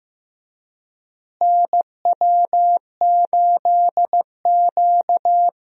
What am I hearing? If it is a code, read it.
Morse code, heard as a keyed tone, about 15 wpm, 700 Hz. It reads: NW8Q
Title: USA Call Signs - 4 Characters - Repeated 3x - 15wpm